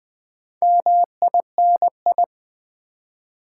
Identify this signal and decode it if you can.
Morse code, heard as a keyed tone, about 20 wpm, 700 Hz. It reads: MINI
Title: Build A Word - 2 Letters At A Time - 8-Letter Words - 20wpm